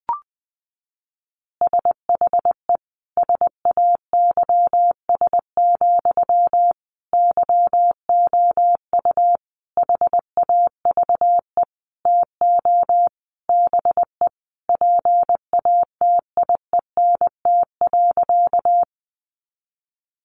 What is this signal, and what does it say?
Morse code, heard as a keyed tone, about 20 wpm, 700 Hz. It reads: SHE SAYS, YOU HAVE TO BE PATIENT.
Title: Easy Phrases 20wpm